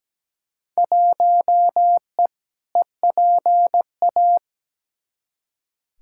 Morse code, heard as a keyed tone, about 17 wpm, 700 Hz. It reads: 1E EPA
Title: Field Day Exchanges - 17wpm